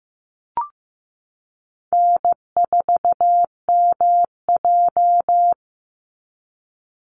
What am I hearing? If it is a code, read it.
Morse code, heard as a keyed tone, about 15 wpm, 700 Hz. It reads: N4MJ
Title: NA SOTA Chaser Callsigns 15wpm